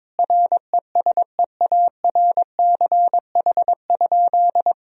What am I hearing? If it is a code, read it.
Morse code, heard as a keyed tone, about 21 wpm, 700 Hz. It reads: RESEARCH?
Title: Sentences from Top 1000 Words - Review Part 4 - 22wpm